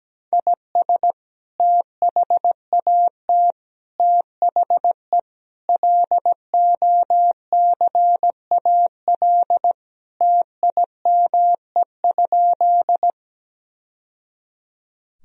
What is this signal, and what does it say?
Morse code, heard as a keyed tone, about 17 wpm, 700 Hz. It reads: IS THAT THE LOCAL TIME?
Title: Sentences from Top 1000 Words - Review Part 3 - 17wpm